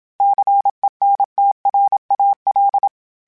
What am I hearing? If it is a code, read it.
Morse code, heard as a keyed tone, about 27 wpm, 800 Hz. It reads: CENTRAL
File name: fromSMs / FromSMs mp3s / Cities@25